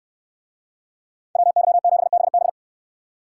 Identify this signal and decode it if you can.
Morse code, heard as a keyed tone, about 68 wpm, 690 Hz. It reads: WY7BB